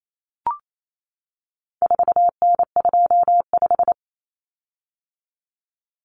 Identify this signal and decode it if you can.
Morse code, heard as a keyed tone, about 28 wpm, 700 Hz. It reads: 4N25
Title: Common Cut Numbers - 4-Digit 28wpm